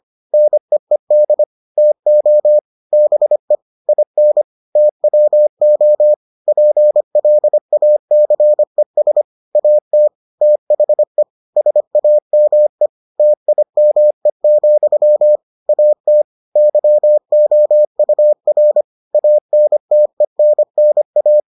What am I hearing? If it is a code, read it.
Morse code, heard as a keyed tone, about 25 wpm, 600 Hz. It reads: NEED TO BE IN TWO PLACES AT THE SAME TIME, AT YOUR ANTENNA